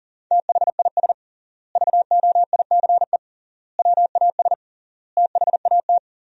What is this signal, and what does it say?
Morse code, heard as a keyed tone, about 40 wpm, 700 Hz. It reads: THIS VOICE WAS THAT
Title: 3x Word-Spacing - Sentences from Top 500 Words - 40wpm